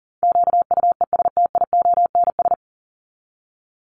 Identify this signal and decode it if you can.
Morse code, heard as a keyed tone, about 40 wpm, 700 Hz. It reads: QUESTIONS
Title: ES:S Suffix - Top 100 Words with Base Words 40wpm